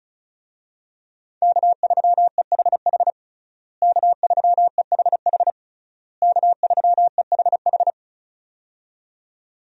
Morse code, heard as a keyed tone, about 36 wpm, 700 Hz. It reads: K3EHH K3EHH K3EHH
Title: USA Call Signs - 5 Characters - Repeated 3x - 35wpm